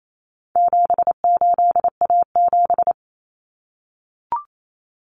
Morse code, heard as a keyed tone, about 28 wpm, 700 Hz. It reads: 78A7
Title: Common Cut Numbers - 4-Digit 28wpm